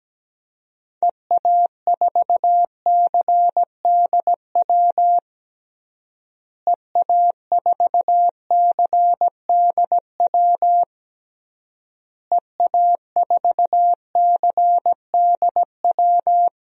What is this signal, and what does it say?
Morse code, heard as a keyed tone, about 17 wpm, 700 Hz. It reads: EA4CDW EA4CDW EA4CDW
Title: DXCC-100 Call Signs - 6 Characters - Repeated 3x - 17wpm